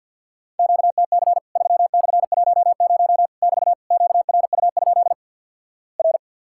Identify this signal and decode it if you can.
Morse code, heard as a keyed tone, about 53 wpm, 700 Hz. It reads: QTC 3/10 <BT> QRU? R
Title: Worked All Europe DX Contest - QTC Exchange - 50wpm